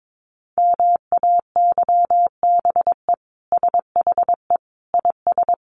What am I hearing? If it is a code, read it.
Morse code, heard as a keyed tone, about 22 wpm, 700 Hz. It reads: MAYBE SHE IS
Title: Sentences from Top 1000 Words 22wpm